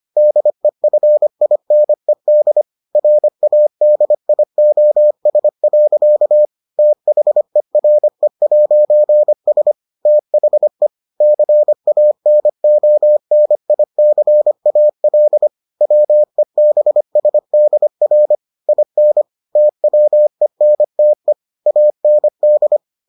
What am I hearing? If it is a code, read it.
Morse code, heard as a keyed tone, about 25 wpm, 600 Hz. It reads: DEFINED RADIOS. THERE'S THE CANONICAL WEBSDR IN TWENTE AND